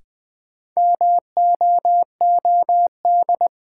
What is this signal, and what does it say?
Morse code, heard as a keyed tone, about 20 wpm, 700 Hz. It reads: MOOD
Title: Sentences from Top 1100 Words - New Word Drill Alt - 20wpm